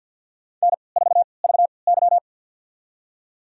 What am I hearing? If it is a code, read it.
Morse code, heard as a keyed tone, about 52 wpm, spaced as for 28 wpm, 700 Hz. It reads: N4VY
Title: USA Call Signs - 4 Characters - 3x Character Spacing - 50wpm